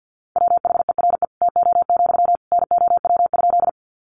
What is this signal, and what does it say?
Morse code, heard as a keyed tone, about 50 wpm, 700 Hz. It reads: WHERE TO, NOW?
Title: Sentences from Top 200 Words 50wpm